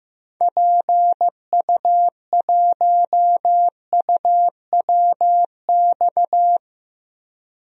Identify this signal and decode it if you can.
Morse code, heard as a keyed tone, about 15 wpm, 700 Hz. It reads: PU1UWX
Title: DXCC-100 Call Signs - 6 Characters - 15wpm